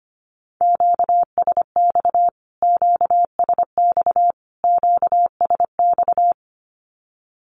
Morse code, heard as a keyed tone, about 25 wpm, 700 Hz. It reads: QSX QSX QSX
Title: Q-Codes Repeated 3x 25wpm